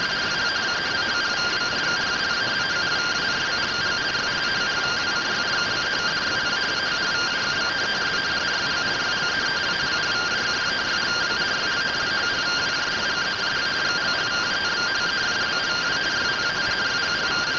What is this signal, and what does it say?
Two-tone teleprinter signal, about 170 Hz shift.
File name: findings / MR AXIAL fr1 120.gen-16 rtty rx